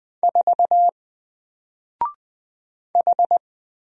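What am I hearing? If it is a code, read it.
Morse code, heard as a keyed tone, about 20 wpm, 700 Hz. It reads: 4 H
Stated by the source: Single letter and number Morse drill at 20 wpm, from N4LCD.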